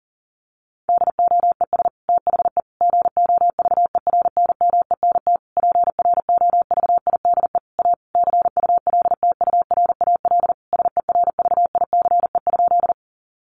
Morse code, heard as a keyed tone, about 40 wpm, 700 Hz. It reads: DOES THE GOVERNMENT PROVIDE A CULTURAL SERVICE?
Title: Sentences from Top 1500 Words 40wpm